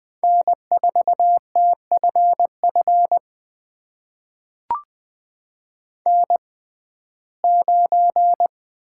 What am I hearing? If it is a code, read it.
Morse code, heard as a keyed tone, about 20 wpm, spaced as for 14 wpm, 700 Hz. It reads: N4TFF N9
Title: USA Call Signs - 5 Characters - 6x Character Spacing - 20wpm